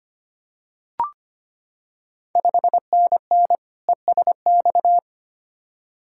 Morse code, heard as a keyed tone, about 25 wpm, 700 Hz. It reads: 5NN ESX
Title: Ontario QSO Party Exchange 25wpm